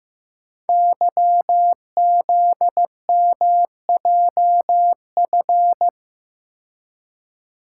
Morse code, heard as a keyed tone, about 15 wpm, 700 Hz. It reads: YZMJF
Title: Code Groups - 5 Letters - 15wpm